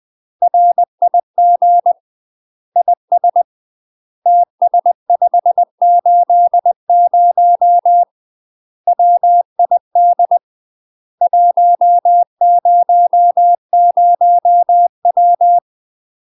Morse code, heard as a keyed tone, about 20 wpm, 700 Hz. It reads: RIG IS TS580 WID 100W
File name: UK MASTER QSOs TRAINING / 20WPM / UK MASTER QSOs 0050